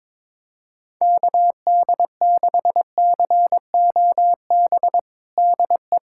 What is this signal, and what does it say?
Morse code, heard as a keyed tone, about 22 wpm, 700 Hz. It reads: KD6COB DE
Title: USA Callsign DE Callsign - 6 Characters - 22wpm